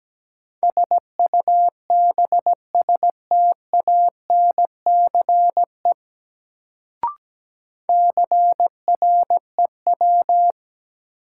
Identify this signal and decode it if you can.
Morse code, heard as a keyed tone, about 17 wpm, 700 Hz. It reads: SUBSTANCE CREW
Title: Sentences from Top 1500 Words - New Word Drill - 17wpm